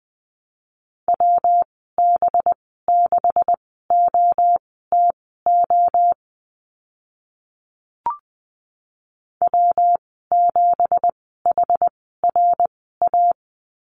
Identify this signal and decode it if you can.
Morse code, heard as a keyed tone, about 20 wpm, spaced as for 14 wpm, 700 Hz. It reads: WB6OTO W7HRA
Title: 186 Lesson 30g - Letter b - Call Signs 20wpm